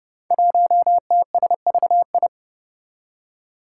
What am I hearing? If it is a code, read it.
Morse code, heard as a keyed tone, about 29 wpm, 700 Hz. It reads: 1TSVI
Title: Five Challenging Letter-Number Combinations 30wpm